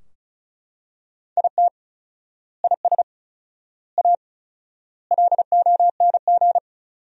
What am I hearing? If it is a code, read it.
Morse code, heard as a keyed tone, about 36 wpm, 700 Hz. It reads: IT IS A LONG
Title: 4x Word-Spacing - Sentences from Top 300 Words - 35wpm